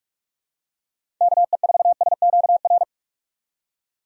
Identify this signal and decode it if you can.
Morse code, heard as a keyed tone, about 46 wpm, 700 Hz. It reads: KE4SQR
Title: USA Call Signs - 6 Characters - 4x Character Spacing - 45wpm